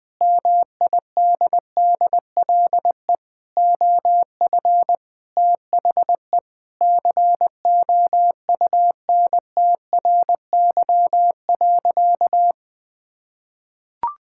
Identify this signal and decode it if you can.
Morse code, heard as a keyed tone, about 20 wpm, 700 Hz. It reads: MIDDLE OF THE COUNTRY.
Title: Sentences from Top 1000 Words - Review Part 6 - 20wpm